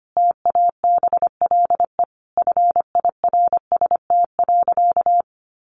Morse code, heard as a keyed tone, about 25 wpm, 700 Hz. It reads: TABLE FIRST.
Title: Sentences from Top 1200 Words 25wpm